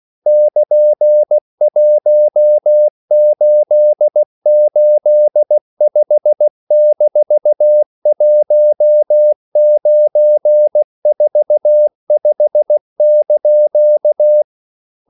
Morse code, heard as a keyed tone, about 16 wpm, 600 Hz. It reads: <KN>1885-1945)
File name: CW / Practice / EnglishQuotations0116